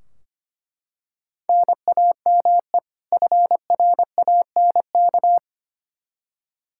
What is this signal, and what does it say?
Morse code, heard as a keyed tone, about 25 wpm, 700 Hz. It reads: NAME FRANK
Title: Practice QSO - Exchange 1 - Beginning - 25wpm